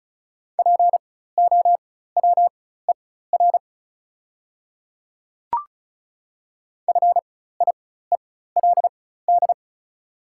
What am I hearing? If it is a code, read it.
Morse code, heard as a keyed tone, about 36 wpm, spaced as for 16 wpm, 700 Hz. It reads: POWER FIELD
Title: 4x Character Spacing - Top 500 Words - 35wpm